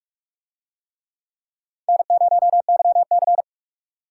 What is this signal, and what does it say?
Morse code, heard as a keyed tone, about 44 wpm, 700 Hz. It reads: N0YC